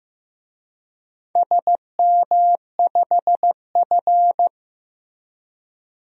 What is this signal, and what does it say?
Morse code, heard as a keyed tone, about 15 wpm, 700 Hz. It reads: SM5F
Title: DXCC-100 Call Signs - 4 Characters - Repeated 2x - 15wpm